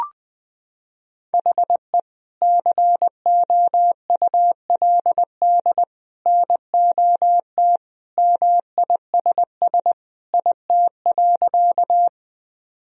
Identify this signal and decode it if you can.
Morse code, heard as a keyed tone, about 21 wpm, 700 Hz. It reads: HE COULD NOT MISS IT.
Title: Sentences from Top 1000 Words - Review Part 3 - 20wpm